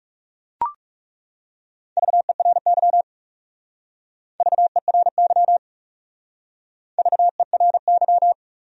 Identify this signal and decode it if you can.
Morse code, heard as a keyed tone, about 38 wpm, 700 Hz. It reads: VERY VERY VERY